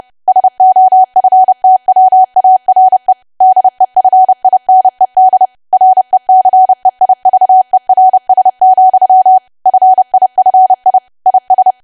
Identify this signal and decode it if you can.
Morse code, heard as a keyed tone, about 30 wpm, 750 Hz. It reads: SOFTWARE DEFINED RECEIVERS, FIFI IS
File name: cwa / W1AW-Level3 / 30WPM